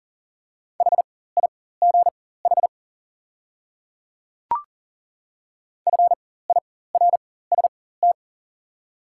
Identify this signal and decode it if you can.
Morse code, heard as a keyed tone, about 37 wpm, spaced as for 18 wpm, 700 Hz. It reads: HIGH FIRST
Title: 4x Character Spacing - Top 200 Words - 40wpm